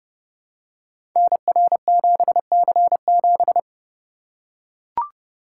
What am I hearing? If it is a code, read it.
Morse code, heard as a keyed tone, about 28 wpm, 700 Hz. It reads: NR7C7
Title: Code Groups - 5 Letter:Numbers - 30wpm